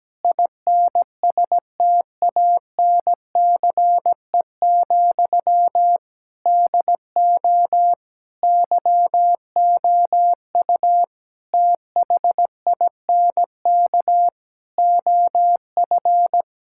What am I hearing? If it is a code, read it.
Morse code, heard as a keyed tone, about 17 wpm, 700 Hz. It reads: INSTANCE, DO YOU THINK OF
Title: Sentences from Top 1000 Words - Review Part 5 - 17wpm